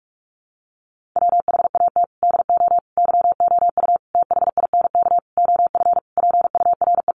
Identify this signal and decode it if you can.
Morse code, heard as a keyed tone, about 44 wpm, 700 Hz. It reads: WHAT DO YOU THINK OF PURE
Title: Sentences from Top 1500 Words 45wpm